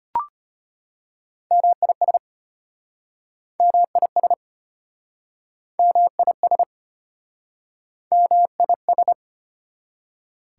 Morse code, heard as a keyed tone, about 33 wpm, 700 Hz. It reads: MIS MIS MIS MIS